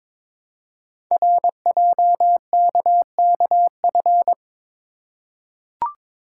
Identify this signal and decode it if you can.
Morse code, heard as a keyed tone, about 23 wpm, 700 Hz. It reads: RJKKF
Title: Code Groups - 5 Letters - 22wpm